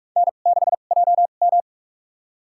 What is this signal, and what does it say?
Morse code, heard as a keyed tone, about 46 wpm, spaced as for 33 wpm, 700 Hz. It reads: N6JM
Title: USA Call Signs - 2x Character Spacing - 45wpm